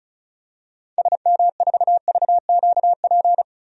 Morse code, heard as a keyed tone, about 36 wpm, 700 Hz. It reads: SM4VQP
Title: DXCC-100 Call Signs - 6 Characters - 35wpm